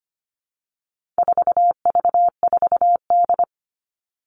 Morse code, heard as a keyed tone, about 25 wpm, 700 Hz. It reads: 4V4D
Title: All Cut Numbers - 4-Digit 25wpm